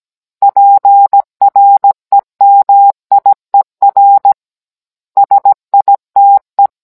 Morse code, heard as a keyed tone, about 17 wpm, 800 Hz. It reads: PREMIER SITE